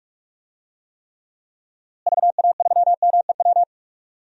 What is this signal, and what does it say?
Morse code, heard as a keyed tone, about 46 wpm, 700 Hz. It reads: VA3MEW
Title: DXCC-100 Call Signs - 6 Characters - Repeated 3x - 45wpm